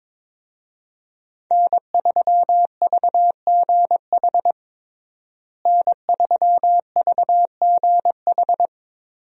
Morse code, heard as a keyed tone, about 22 wpm, 690 Hz. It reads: N3VGH N3VGH
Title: USA Call Signs - 5 Characters - Repeated 3x - 22wpm